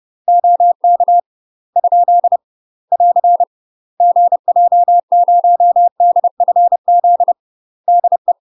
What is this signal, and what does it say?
Morse code, heard as a keyed tone, about 30 wpm, 700 Hz. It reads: OK ? <AR> GJ0DFZ DE
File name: UK MASTER QSOs TRAINING / 30WPM / UK MASTER QSOs 0029